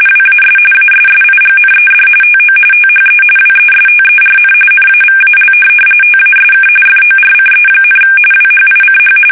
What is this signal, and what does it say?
RTTY, about 45.45 baud. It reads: WAS ASSIGNED TO THE WESTERN NAVAL TASK FORCE, WHICH
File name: fsk850 / USS-Massachusetts